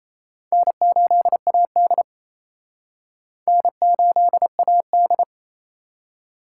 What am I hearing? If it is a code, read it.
Morse code, heard as a keyed tone, about 32 wpm, 700 Hz. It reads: N8AD N8AD
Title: Speed Racing - USA 1x2 Call Signs 28wpm